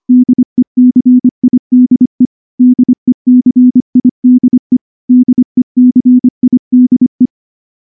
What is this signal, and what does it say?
Morse code, heard as a keyed tone, about 24 wpm, 250 Hz. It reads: DECIDE DECIDE DECIDE
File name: morse / morse-training-25WPM